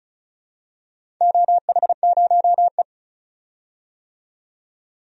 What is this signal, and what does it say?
Morse code, heard as a keyed tone, about 36 wpm, 700 Hz. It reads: OH0E
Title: DXCC-100 Call Signs - 4 Characters - 35wpm